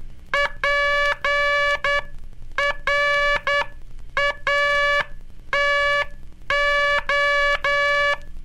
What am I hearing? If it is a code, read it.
Morse code, heard as a keyed tone, about 10 wpm, spaced as for 8 wpm, 1700 Hz, with noise. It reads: PRATO
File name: cursos / Curso CW mp3 / CW Aula - 15